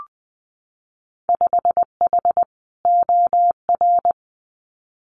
Morse code, heard as a keyed tone, about 20 wpm, 700 Hz. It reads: 5H OR
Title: Winter Field Day Exchanges Encore 20wpm